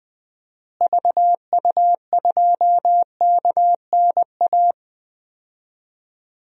Morse code, heard as a keyed tone, about 20 wpm, 700 Hz. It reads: VU2KNA
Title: DXCC-100 Call Signs - 6 Characters - Repeated 2x - 20wpm